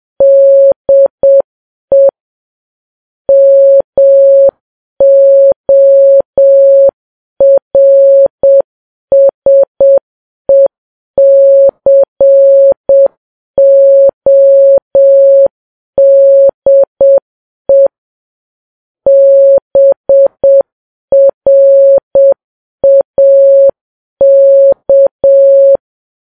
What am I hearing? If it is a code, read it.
Morse code, heard as a keyed tone, about 7 wpm, 550 Hz. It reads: DE MORSECODE BRAK